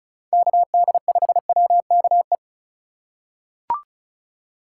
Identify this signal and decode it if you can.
Morse code, heard as a keyed tone, about 36 wpm, 700 Hz. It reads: KD5WKE
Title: USA Call Signs - 6 Characters - 2x Character Spacing - 35wpm